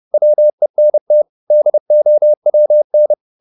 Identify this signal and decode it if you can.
Morse code, heard as a keyed tone, about 30 wpm, 600 Hz. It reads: WENT DOWN
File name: the old man and the sea / the old man and the sea 1 30wpm